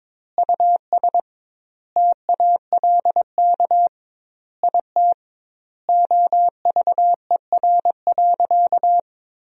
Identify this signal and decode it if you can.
Morse code, heard as a keyed tone, about 21 wpm, 700 Hz. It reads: US TALK IT OVER.
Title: 2x Word-Spacing - Sentences from Top 500 Words - 22wpm